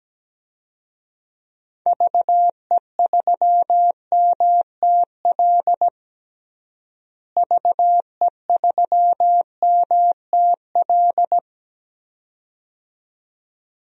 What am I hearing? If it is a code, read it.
Morse code, heard as a keyed tone, about 17 wpm, 700 Hz. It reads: VE3MTL VE3MTL
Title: DXCC-100 Call Signs - 6 Characters - Repeated 2x - 17wpm